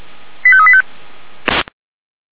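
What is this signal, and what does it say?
Five-tone selective call (ZVEI), heard as quick runs of five tones, tones of about 70 ms. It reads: 85385
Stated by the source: Five-tone ZVEI selective call with code 85392, received on an SDR.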